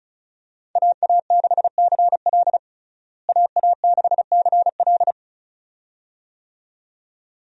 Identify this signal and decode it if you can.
Morse code, heard as a keyed tone, about 34 wpm, 700 Hz. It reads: AA6CL AA6CL
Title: USA Call Signs - 5 Characters - Repeated 2x - 35wpm